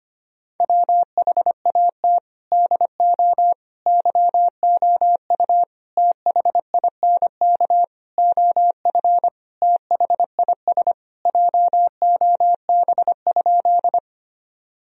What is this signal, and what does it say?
Morse code, heard as a keyed tone, about 25 wpm, 700 Hz. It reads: WHAT DO YOU THINK OF THIS JOB?
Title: Sentences from Top 1000 Words - Review Part 3 - 25wpm